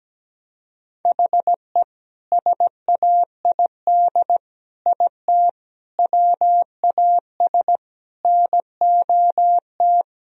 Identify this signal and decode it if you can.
Morse code, heard as a keyed tone, about 17 wpm, 700 Hz. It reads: HE SAID IT WAS NOT
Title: Sentences from Top 1000 Words - Review Part 2 - 17wpm